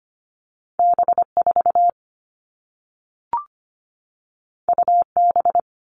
Morse code, heard as a keyed tone, about 25 wpm, 700 Hz. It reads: B4 UB
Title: All Cut Numbers - 2-Digit 25wpm